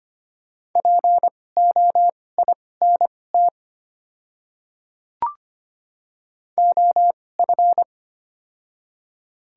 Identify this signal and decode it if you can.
Morse code, heard as a keyed tone, about 25 wpm, spaced as for 18 wpm, 700 Hz. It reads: POINT OF
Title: 2x Character Spacing - Top 200 Words - 25wpm